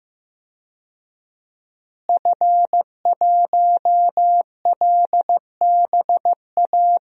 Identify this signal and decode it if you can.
Morse code, heard as a keyed tone, about 15 wpm, 700 Hz. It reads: F1LBA
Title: DXCC-100 Call Signs - 5 Characters - Repeated 2x - 15wpm